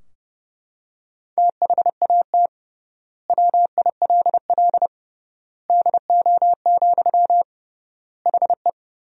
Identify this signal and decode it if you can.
Morse code, heard as a keyed tone, about 32 wpm, 700 Hz. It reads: THAT WILL DO, HE
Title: 3x Word-Spacing - Sentences from Top 100 Words - 30wpm